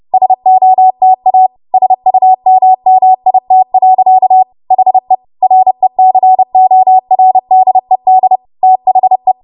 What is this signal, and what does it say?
Morse code, heard as a keyed tone, about 30 wpm, 750 Hz. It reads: SOTA SUMMIT. HE RECORDED THE
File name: Morse / 30WPM